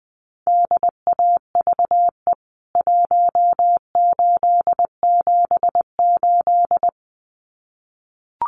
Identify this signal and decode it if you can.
Morse code, heard as a keyed tone, about 19 wpm, 700 Hz. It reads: DAVE 1878
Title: CWT Exchange 20wpm